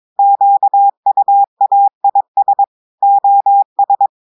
Morse code, heard as a keyed tone, about 22 wpm, 800 Hz. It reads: QUAIS OS
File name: audios / portugais / portugais texte en clair 2 vitesse 22 mots minutes